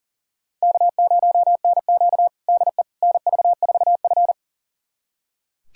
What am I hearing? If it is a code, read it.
Morse code, heard as a keyed tone, about 40 wpm, 700 Hz. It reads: K0NQ DE NV4F
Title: USA Callsign DE Callsign - 4 Characters - 40wpm